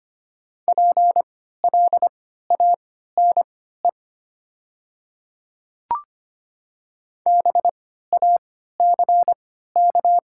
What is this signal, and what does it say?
Morse code, heard as a keyed tone, about 25 wpm, spaced as for 14 wpm, 700 Hz. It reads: PLANE BACK
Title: 3x Character Spacing - Top 500 Words - 25wpm